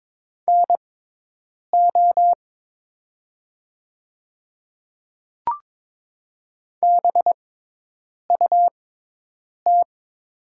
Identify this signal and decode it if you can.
Morse code, heard as a keyed tone, about 22 wpm, spaced as for 8 wpm, 700 Hz. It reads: NO BUT
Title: 6x Character Spacing - Top 100 Words - 22wpm